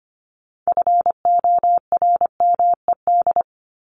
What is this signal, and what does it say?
Morse code, heard as a keyed tone, about 25 wpm, 700 Hz. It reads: FORMED
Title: ED Suffix - Top 100 Words with Base Words 25wpm